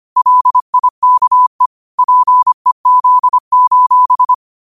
Morse code, heard as a keyed tone, about 25 wpm, 1000 Hz. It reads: LIKE PEZ:
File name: qotdfiles / 25wpm